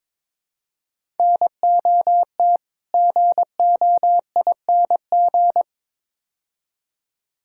Morse code, heard as a keyed tone, about 22 wpm, 700 Hz. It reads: NOT GOING
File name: practice / 22wpm / Build A Phrase - 5-Word Phrases - 22wpm